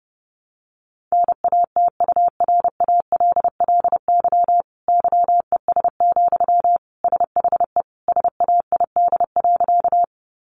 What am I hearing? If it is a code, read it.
Morse code, heard as a keyed tone, about 30 wpm, 700 Hz. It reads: NATURALLY YES, SHE SAID.
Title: Sentences from Top 1300 Words 30wpm